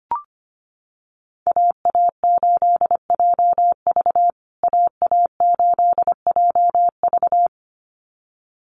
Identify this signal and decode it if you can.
Morse code, heard as a keyed tone, about 25 wpm, 700 Hz. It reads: AA8JV AA8JV
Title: USA Call Signs Repeated 2x 25wpm